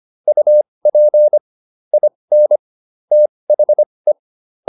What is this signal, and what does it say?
Morse code, heard as a keyed tone, about 25 wpm, spaced as for 20 wpm, 600 Hz. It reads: UP IN THE